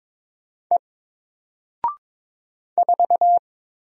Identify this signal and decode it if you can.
Morse code, heard as a keyed tone, about 22 wpm, 700 Hz. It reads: E 4
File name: Single-Letter-Number-22wpm